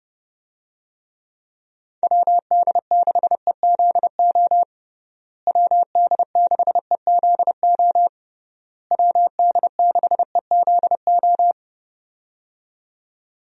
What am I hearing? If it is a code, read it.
Morse code, heard as a keyed tone, about 30 wpm, 700 Hz. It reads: WD6EZO WD6EZO WD6EZO
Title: USA Call Signs - 6 Characters - Repeated 3x - 30wpm